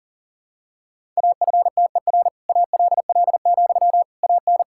Morse code, heard as a keyed tone, about 40 wpm, 700 Hz. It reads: AFTER ALL, AN
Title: Sentences from Top 1000 Words - Review Part 6 - 40wpm